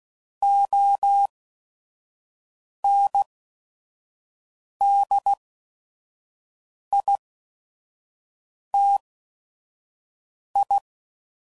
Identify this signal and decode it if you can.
Morse code, heard as a keyed tone, about 16 wpm, spaced as for 5 wpm, 800 Hz. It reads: ONDITI
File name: CW Code Groups @5WPM / Technical Subjects / AND Gate